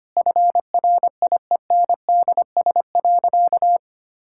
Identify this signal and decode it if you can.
Morse code, heard as a keyed tone, about 25 wpm, 700 Hz. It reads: FRIENDS.
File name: practice / 25wpm / Sentences with 4-Word Phrases - 25wpm